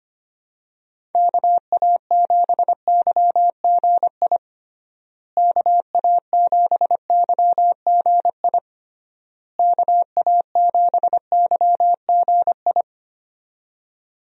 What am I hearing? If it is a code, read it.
Morse code, heard as a keyed tone, about 25 wpm, 700 Hz. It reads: KA7YGI KA7YGI KA7YGI
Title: USA Call Signs - 6 Characters - Repeated 3x - 25wpm